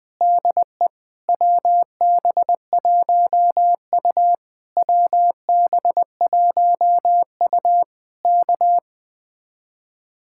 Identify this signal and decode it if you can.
Morse code, heard as a keyed tone, about 20 wpm, 700 Hz. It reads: DE WB1U WB1U K
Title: Speed-Racing Common QSO Phrases - 17wpm